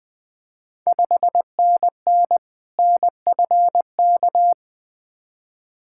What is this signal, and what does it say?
Morse code, heard as a keyed tone, about 20 wpm, 690 Hz. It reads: 5NN NFK
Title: Ontario QSO Party Exchange 20wpm